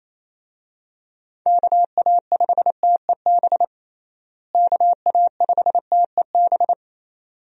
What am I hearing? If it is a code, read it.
Morse code, heard as a keyed tone, about 28 wpm, 700 Hz. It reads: KA5TEB KA5TEB
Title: USA Call Signs - 6 Characters - Repeated 2x - 28wpm